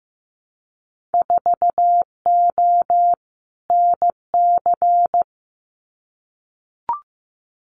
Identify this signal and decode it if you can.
Morse code, heard as a keyed tone, about 15 wpm, 700 Hz. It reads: 4O NC
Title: Winter Field Day Exchanges 15wpm